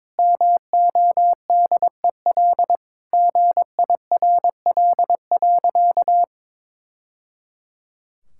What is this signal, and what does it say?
Morse code, heard as a keyed tone, about 22 wpm, 700 Hz. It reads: MODEL GIRL.
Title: Sentences from Top 1000 Words - Review Part 5 - 22wpm